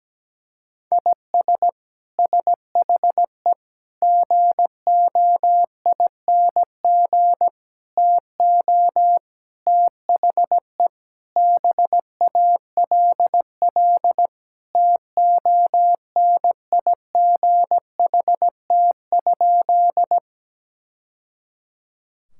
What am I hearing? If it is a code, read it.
Morse code, heard as a keyed tone, about 17 wpm, 700 Hz. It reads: IS SHE GOING TO THE BALL TONIGHT?
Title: Sentences from Top 1000 Words - Review Part 6 - 17wpm